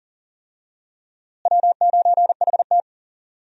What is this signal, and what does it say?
Morse code, heard as a keyed tone, about 37 wpm, 700 Hz. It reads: W9HT
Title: USA Call Signs - 4 Characters - Repeated 3x - 40wpm